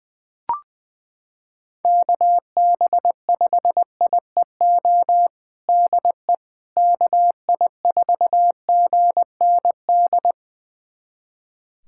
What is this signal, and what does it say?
Morse code, heard as a keyed tone, about 20 wpm, 700 Hz. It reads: KB5IEO DE KI4GND
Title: USA Callsign DE Callsign 20wpm